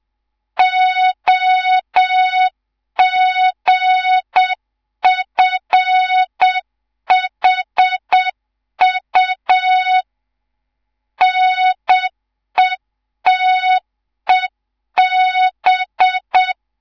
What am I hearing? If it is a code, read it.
Morse code, heard as a keyed tone, about 7 wpm, 3700 Hz. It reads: OGFHU NETEB
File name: Morze-HA2MN / hang / 11 GYAKORLAT